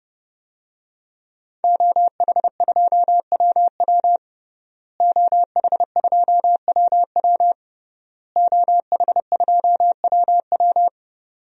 Morse code, heard as a keyed tone, about 30 wpm, 700 Hz. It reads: OH2WW OH2WW OH2WW
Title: DXCC-100 Call Signs - 5 Characters - Repeated 3x - 30wpm